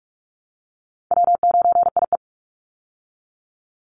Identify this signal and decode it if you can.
Morse code, heard as a keyed tone, about 46 wpm, 700 Hz. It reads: W9IE
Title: Easy Call Signs Repeated 3x - 45wpm